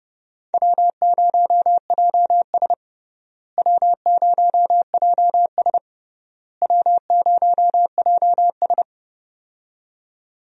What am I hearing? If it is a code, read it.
Morse code, heard as a keyed tone, about 30 wpm, 700 Hz. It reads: W0JS W0JS W0JS
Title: USA Call Signs - 4 Characters - Repeated 3x - 30wpm